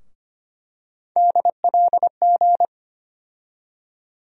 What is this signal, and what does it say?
Morse code, heard as a keyed tone, about 25 wpm, 700 Hz. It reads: DLG
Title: USA Call Sign Postfixes - 3 Letters 25wpm